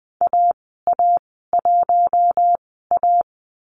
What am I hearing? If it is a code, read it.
Morse code, heard as a keyed tone, about 20 wpm, spaced as for 14 wpm, 700 Hz. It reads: AA1A
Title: 048 Lesson 9g - Number 4 - Call Signs 20wpm